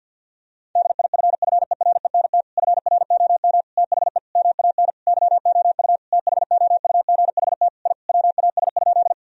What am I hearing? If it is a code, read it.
Morse code, heard as a keyed tone, about 50 wpm, 700 Hz. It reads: DIFFERENT FROM THE MAN YOU THOUGHT I WAS?
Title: Sentences from Top 1000 Words - Review Part 3 - 50wpm